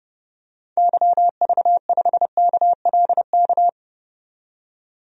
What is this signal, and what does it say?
Morse code, heard as a keyed tone, about 31 wpm, 700 Hz. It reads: YV5KLK
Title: DXCC-100 Call Signs - 6 Characters - 30wpm